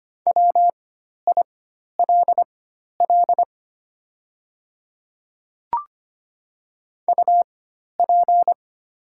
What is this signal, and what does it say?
Morse code, heard as a keyed tone, about 25 wpm, spaced as for 12 wpm, 700 Hz. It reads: WILL UP
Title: 4x Character Spacing - Top 200 Words - 25wpm